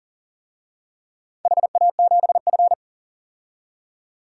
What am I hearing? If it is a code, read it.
Morse code, heard as a keyed tone, about 40 wpm, 700 Hz. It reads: HA7F
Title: DXCC-100 Call Signs - 4 Characters - 40wpm